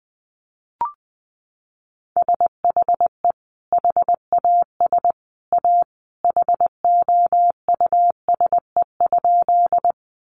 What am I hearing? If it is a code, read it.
Morse code, heard as a keyed tone, about 20 wpm, 700 Hz. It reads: SHE HAS A HOUSE?
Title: Sentences from Top 200 Words 20wpm